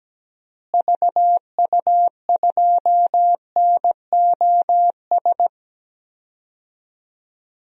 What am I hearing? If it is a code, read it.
Morse code, heard as a keyed tone, about 17 wpm, 700 Hz. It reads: VU2NOS
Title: DXCC-100 Call Signs - 6 Characters - Repeated 2x - 17wpm